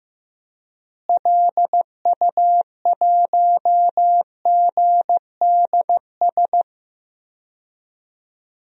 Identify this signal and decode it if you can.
Morse code, heard as a keyed tone, about 15 wpm, 700 Hz. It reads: LU1GDS
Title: DXCC-100 Call Signs - 6 Characters - Repeated 3x - 15wpm